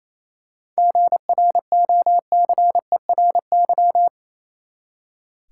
Morse code, heard as a keyed tone, about 30 wpm, 700 Hz. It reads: GROCERY
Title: Seven Letter Words - 28wpm